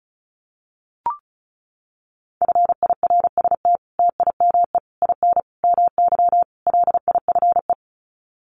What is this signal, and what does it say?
Morse code, heard as a keyed tone, about 34 wpm, 700 Hz. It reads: FIRST TIME IN MY LIFE
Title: Sets of 5 Words Encore2 35wpm